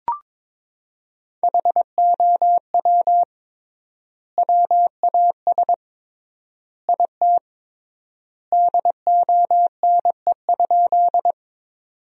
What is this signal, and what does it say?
Morse code, heard as a keyed tone, about 22 wpm, 700 Hz. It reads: HOW WAS IT DONE?
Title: 3x Word-Spacing - Sentences from Top 500 Words - 22wpm